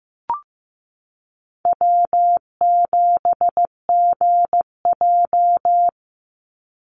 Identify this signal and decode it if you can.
Morse code, heard as a keyed tone, about 15 wpm, 700 Hz. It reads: W7GJ
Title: NA SOTA Chaser Callsigns 15wpm